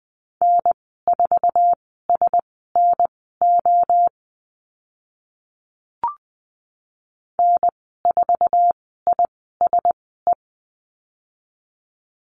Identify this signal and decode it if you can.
Morse code, heard as a keyed tone, about 20 wpm, spaced as for 14 wpm, 700 Hz. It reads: N4SNO N4ISE
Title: 069 Lesson 12g - Letter d - Call Signs 20wpm